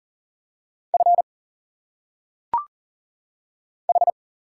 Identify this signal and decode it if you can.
Morse code, heard as a keyed tone, about 40 wpm, 700 Hz. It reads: F H
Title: Single Letters - Phonetic Pronunciation 40wpm